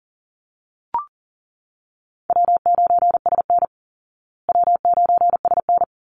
Morse code, heard as a keyed tone, about 40 wpm, 700 Hz. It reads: W9SN W9SN
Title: Easy Call Signs Repeated 2x - 40wpm